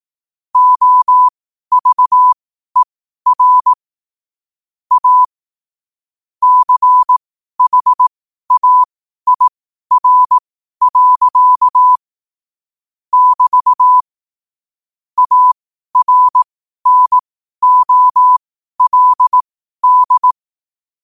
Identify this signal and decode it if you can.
Morse code, heard as a keyed tone, about 18 wpm, spaced as for 12 wpm, 1000 Hz. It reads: OVER A CHAIR. <BT> ARNOLD
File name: qotdfiles / 12wpm